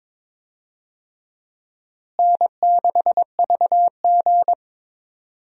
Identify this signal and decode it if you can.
Morse code, heard as a keyed tone, about 22 wpm, 700 Hz. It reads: N6VG
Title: USA Call Signs - 4 Characters - Repeated 3x - 22wpm